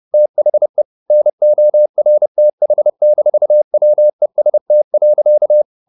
Morse code, heard as a keyed tone, about 30 wpm, 600 Hz. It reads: THE NORTH-WEST.
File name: the old man and the sea / the old man and the sea 1 30wpm